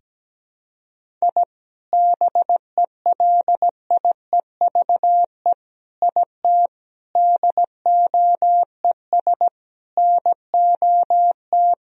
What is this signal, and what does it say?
Morse code, heard as a keyed tone, about 17 wpm, 700 Hz. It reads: I BELIEVE IT DOES NOT
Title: Sentences from Top 1000 Words - Review Part 2 - 17wpm